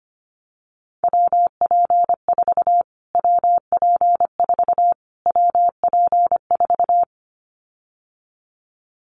Morse code, heard as a keyed tone, about 25 wpm, 700 Hz. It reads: WP4 WP4 WP4
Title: Callsign Prefixes for Top 25 DXCC - Repeated 3x - 25wpm